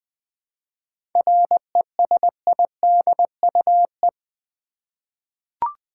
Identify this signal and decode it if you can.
Morse code, heard as a keyed tone, about 20 wpm, 700 Hz. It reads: RESIDUE
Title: 7 Letter Words - First 3 Letters Given - 20wpm